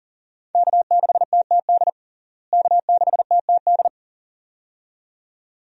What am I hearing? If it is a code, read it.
Morse code, heard as a keyed tone, about 40 wpm, 700 Hz. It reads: K6TTD K6TTD
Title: USA Call Signs - 5 Characters - Repeated 2x - 40wpm